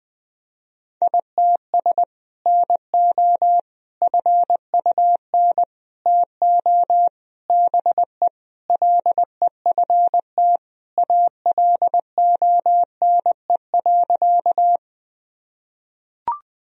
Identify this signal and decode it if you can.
Morse code, heard as a keyed tone, about 21 wpm, 700 Hz. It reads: ITS NO FUN TO BE LEFT ALONE.
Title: Sentences from Top 1000 Words - Review Part 4 - 20wpm